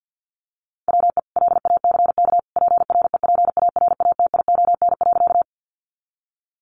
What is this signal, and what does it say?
Morse code, heard as a keyed tone, about 50 wpm, 700 Hz. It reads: WE LACK PREPARATION.
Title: Sentences from Top 1400 Words 50wpm